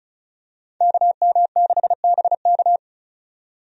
Morse code, heard as a keyed tone, about 36 wpm, 700 Hz. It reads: KM6BK
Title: USA Call Signs - 5 Characters - 35wpm